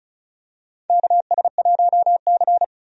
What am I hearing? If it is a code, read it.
Morse code, heard as a keyed tone, about 34 wpm, 700 Hz. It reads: KS1C
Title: USA Call Signs - 4 Characters - 35wpm